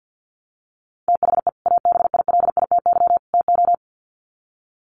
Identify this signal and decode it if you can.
Morse code, heard as a keyed tone, about 49 wpm, 700 Hz. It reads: THE ABILITY TO
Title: Sets of 3 Words 50wpm